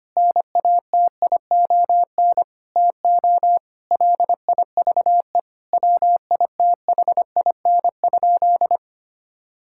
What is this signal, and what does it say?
Morse code, heard as a keyed tone, about 25 wpm, 700 Hz. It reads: NATION TO LIVE WITHIN?
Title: Sentences from Top 1000 Words - Review Part 6 - 25wpm